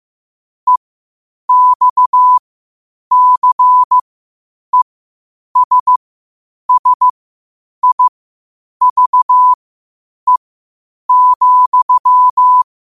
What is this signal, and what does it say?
Morse code, heard as a keyed tone, about 15 wpm, spaced as for 8 wpm, 1000 Hz. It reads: EXCESSIVE,
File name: qotdfiles / 8wpm